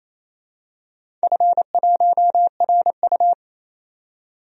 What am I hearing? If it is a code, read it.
Morse code, heard as a keyed tone, about 28 wpm, 700 Hz. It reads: F1RU
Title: DXCC-100 Call Signs - 4 Characters - 28wpm